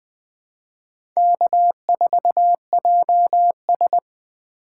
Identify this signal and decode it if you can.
Morse code, heard as a keyed tone, about 20 wpm, 700 Hz. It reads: K4JS
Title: USA Call Signs - 4 Characters - 2x Character Spacing - 20wpm